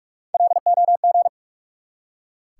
Morse code, heard as a keyed tone, about 44 wpm, 700 Hz. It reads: LOG
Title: Three Letter Words - 45wpm